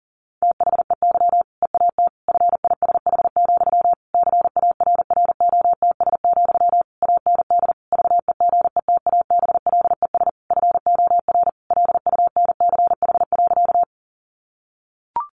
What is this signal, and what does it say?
Morse code, heard as a keyed tone, about 40 wpm, 700 Hz. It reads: THEY EAT FISH, CARROTS, AND VEGETABLES FOR LUNCH.
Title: Easy Phrases 40wpm